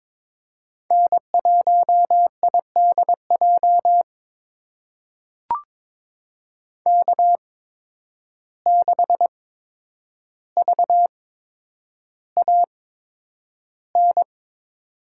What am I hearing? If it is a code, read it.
Morse code, heard as a keyed tone, about 22 wpm, spaced as for 9 wpm, 700 Hz. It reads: N1IDJ K6VAN
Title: USA Call Signs - 5 Characters - 8x Character Spacing - 22wpm